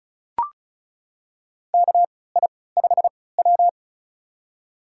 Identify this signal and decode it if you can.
Morse code, heard as a keyed tone, about 36 wpm, spaced as for 20 wpm, 700 Hz. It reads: KI5W
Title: USA Call Signs - 3x Character Spacing - 35wpm